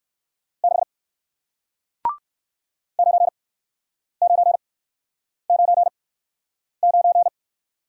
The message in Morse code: 6 9999

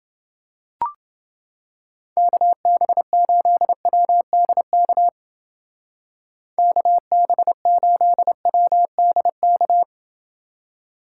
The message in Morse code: KB8WDK KB8WDK

